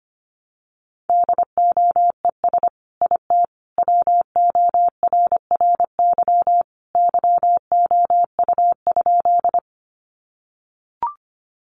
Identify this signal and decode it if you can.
Morse code, heard as a keyed tone, about 25 wpm, 700 Hz. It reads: DOES IT WORRY YOU?
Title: Sentences from Top 1100 Words 25wpm